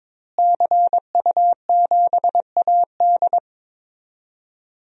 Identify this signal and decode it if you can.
Morse code, heard as a keyed tone, about 22 wpm, 700 Hz. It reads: CU7AD
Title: DXCC-100 Call Signs - 5 Characters - Repeated 3x - 22wpm